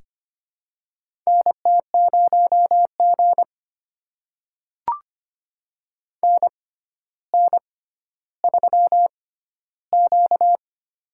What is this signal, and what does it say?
Morse code, heard as a keyed tone, about 25 wpm, spaced as for 13 wpm, 700 Hz. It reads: NT0G NN3Q
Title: USA Call Signs - 4 Characters - 6x Character Spacing - 25wpm